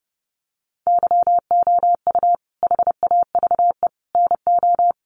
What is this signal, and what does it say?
Morse code, heard as a keyed tone, about 30 wpm, 700 Hz. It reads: YOU HAVE NO
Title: Sentences from Top 800 Words 30wpm